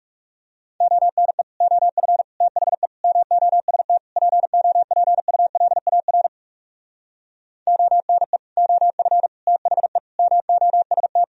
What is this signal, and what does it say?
Morse code, heard as a keyed tone, about 37 wpm, 700 Hz. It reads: ONE OF THE MOST POPULAR ONE OF THE MOST
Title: Speed-Racing - Sets of 5 Words 30wpm